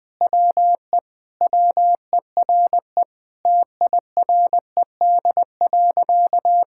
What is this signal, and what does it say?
Morse code, heard as a keyed tone, about 20 wpm, 700 Hz. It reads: WE WERE TIRED.
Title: Sentences with 5-Word Phrases - 20wpm